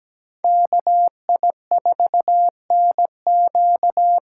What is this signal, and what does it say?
Morse code, heard as a keyed tone, about 17 wpm, 700 Hz. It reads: KI4NQ
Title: USA Call Signs - 6 Characters - 8x Character Spacing - 17wpm